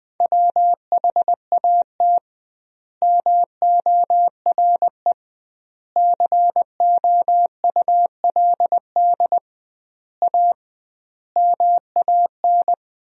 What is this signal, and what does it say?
Morse code, heard as a keyed tone, about 20 wpm, 700 Hz. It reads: WHAT MORE COULD A MAN
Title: 2x Word-Spacing - Sentences from Top 200 Words - 20wpm